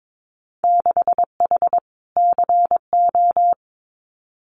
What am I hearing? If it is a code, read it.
Morse code, heard as a keyed tone, about 22 wpm, 700 Hz. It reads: 6H CO
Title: Winter Field Day Exchanges Encore 22wpm